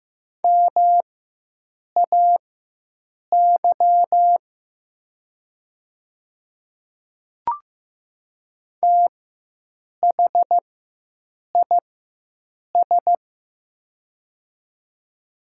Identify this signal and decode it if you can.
Morse code, heard as a keyed tone, about 15 wpm, spaced as for 7 wpm, 700 Hz. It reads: MAY THIS
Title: 4x Character Spacing - Top 100 Words - 15wpm